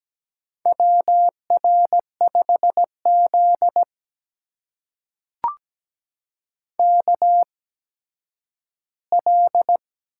Morse code, heard as a keyed tone, about 17 wpm, spaced as for 10 wpm, 700 Hz. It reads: WR5Z KL
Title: USA Call Signs - 8x Character Spacing - 17wpm